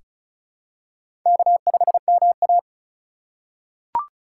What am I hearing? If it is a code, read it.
Morse code, heard as a keyed tone, about 34 wpm, 700 Hz. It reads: K5MA
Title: K1USN SST QSO Practice - 35wpm